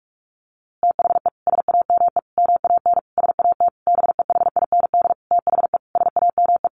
Morse code, heard as a keyed tone, about 44 wpm, 700 Hz. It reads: THE SAME MAN SAT BEHIND THE SAME
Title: A Sound Of Thunder 45wpm